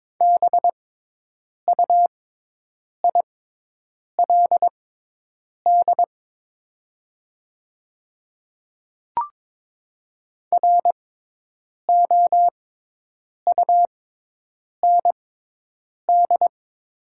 Morse code, heard as a keyed tone, about 23 wpm, spaced as for 8 wpm, 700 Hz. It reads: BUILD ROUND
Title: 6x Character Spacing - Top 300 Words - 22wpm